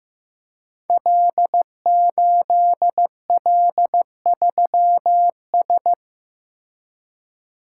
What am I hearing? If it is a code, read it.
Morse code, heard as a keyed tone, about 15 wpm, 700 Hz. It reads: L8L3S
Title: Code Groups - 5 Letter:Numbers - 15wpm